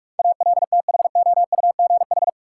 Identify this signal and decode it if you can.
Morse code, heard as a keyed tone, about 46 wpm, 700 Hz. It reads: ALTHOUGH
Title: Sentences from Top 600 Words - New Word Drill - 45wpm